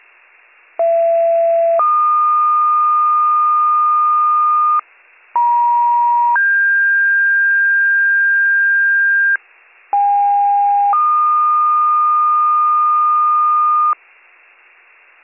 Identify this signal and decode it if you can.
Two-tone page, heard as a one-second tone, then a three-second tone, about 670 then 1220 Hz, about 950 then 1600 Hz, about 810 then 1220 Hz. It is heard from 0.8 to 13.9 s.